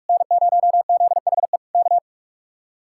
Morse code, heard as a keyed tone, about 49 wpm, 700 Hz. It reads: N0ZHE K